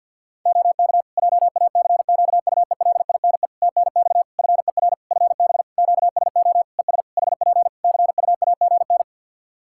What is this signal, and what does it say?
Morse code, heard as a keyed tone, about 53 wpm, 700 Hz. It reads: OK JACQUELINE TNX FER FB QSO ES HP CUAGN
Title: K1USN SST QSO Practice 50wpm